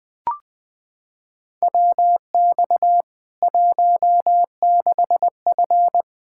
Morse code, heard as a keyed tone, about 20 wpm, 700 Hz. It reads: WX 16F
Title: K1USN SST QSO Practice 20wpm